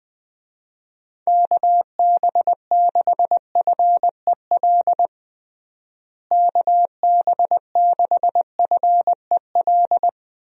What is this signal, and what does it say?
Morse code, heard as a keyed tone, about 21 wpm, 700 Hz. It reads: KB6FEL KB6FEL
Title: USA Call Signs - 6 Characters - Repeated 2x - 20wpm